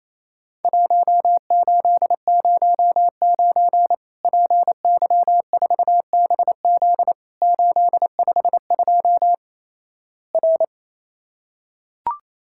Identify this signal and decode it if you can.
Morse code, heard as a keyed tone, about 28 wpm, 700 Hz. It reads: 1809 PY4BZ 852 R
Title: Worked All Europe DX Contest - QTC Exchange - 28wpm